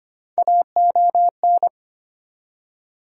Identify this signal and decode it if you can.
Morse code, heard as a keyed tone, about 25 wpm, 700 Hz. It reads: AON
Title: USA Call Sign Postfixes - 3 Letters 25wpm